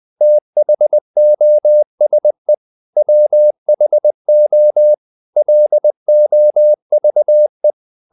Morse code, heard as a keyed tone, about 20 wpm, 600 Hz. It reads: THOSE WHO LOVE
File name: the old man and the sea / the old man and the sea part 1 20wpm